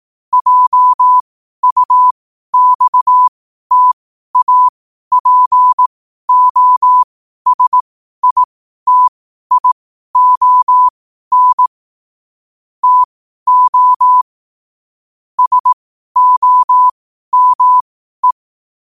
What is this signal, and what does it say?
Morse code, heard as a keyed tone, about 18 wpm, spaced as for 12 wpm, 1000 Hz. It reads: JUXTAPOSITION TO SOME
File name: qotdfiles / 12wpm